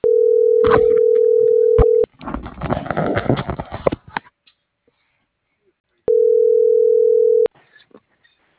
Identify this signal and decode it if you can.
Ringback tone (North America), 440 + 480 Hz.